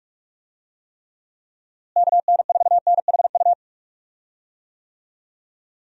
Morse code, heard as a keyed tone, about 46 wpm, 700 Hz. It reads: KN4NHU
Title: USA Call Signs - 6 Characters - Repeated 2x - 45wpm